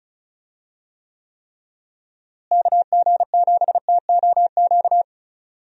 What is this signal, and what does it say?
Morse code, heard as a keyed tone, about 34 wpm, 700 Hz. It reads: KG7TOQ
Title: USA Call Signs - 6 Characters - Repeated 3x - 35wpm